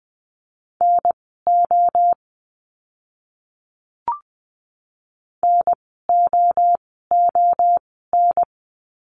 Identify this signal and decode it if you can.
Morse code, heard as a keyed tone, about 20 wpm, spaced as for 15 wpm, 700 Hz. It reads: NO NOON
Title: 024 Lesson 5f - Letter o - Words 20wpm